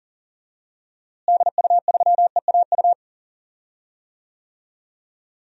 Morse code, heard as a keyed tone, about 40 wpm, 700 Hz. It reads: DU3EAU
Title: DXCC-100 Call Signs - 6 Characters - Repeated 2x - 40wpm